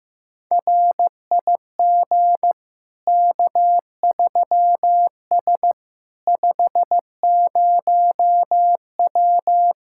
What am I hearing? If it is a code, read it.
Morse code, heard as a keyed tone, about 15 wpm, 700 Hz. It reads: RIG K3S 50W
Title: Practice QSO - Exchange 2 - Info - 15wpm